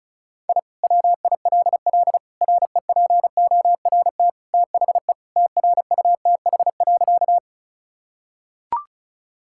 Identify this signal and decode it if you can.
Morse code, heard as a keyed tone, about 36 wpm, 700 Hz. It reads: I WILL REPORT THE TRUTH.
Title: Sentences from Top 1000 Words - Review Part 6 - 35wpm